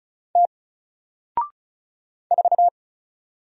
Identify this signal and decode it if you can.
Morse code, heard as a keyed tone, about 36 wpm, 700 Hz. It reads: T 4
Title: Single Character (Letter:Number:Punctuation:Prosign) 35wpm